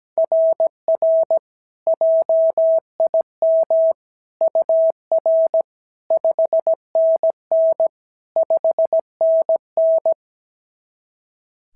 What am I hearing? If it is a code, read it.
Morse code, heard as a keyed tone, about 16 wpm, 650 Hz. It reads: RR JIM UR 5NN 5NN
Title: SOTA QSO Practice 17wpm